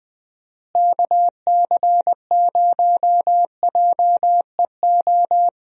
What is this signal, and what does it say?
Morse code, heard as a keyed tone, about 20 wpm, 700 Hz. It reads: KC0JEO
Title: USA Call Signs - 6 Characters - Repeated 3x - 20wpm